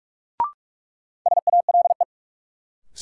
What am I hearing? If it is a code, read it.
Morse code, heard as a keyed tone, about 46 wpm, 700 Hz. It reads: SALE
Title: Sentences from Top 1400 Words - New Word Drill - 45wpm